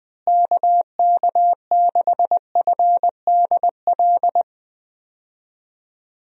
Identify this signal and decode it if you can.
Morse code, heard as a keyed tone, about 20 wpm, 700 Hz. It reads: KK6FDL
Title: USA Call Signs - 6 Characters - Repeated 3x - 20wpm